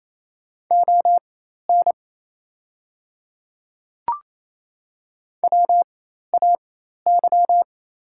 Morse code, heard as a keyed tone, about 27 wpm, spaced as for 13 wpm, 690 Hz. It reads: ON WAY
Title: 4x Character Spacing - Top 200 Words - 28wpm